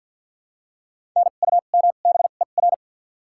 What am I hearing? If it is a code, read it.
Morse code, heard as a keyed tone, about 48 wpm, 700 Hz. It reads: N U M B E R
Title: 2x Character Spacing - Top 500 Words - 50wpm